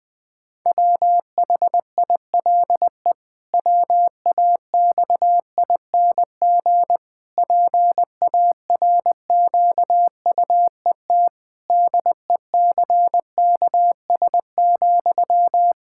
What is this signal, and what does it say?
Morse code, heard as a keyed tone, about 20 wpm, 700 Hz. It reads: WHILE WAXING PARQUET DECKS,
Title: Pangram Challenge v2 - 20wpm